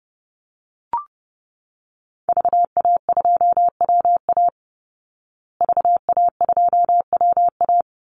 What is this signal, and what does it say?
Morse code, heard as a keyed tone, about 30 wpm, 700 Hz. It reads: VA2WA VA2WA
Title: Call Signs Repeated 2x - 30wpm